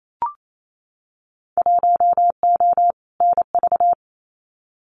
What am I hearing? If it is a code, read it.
Morse code, heard as a keyed tone, about 28 wpm, 700 Hz. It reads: 1O NV
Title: Winter Field Day Exchanges Encore 28wpm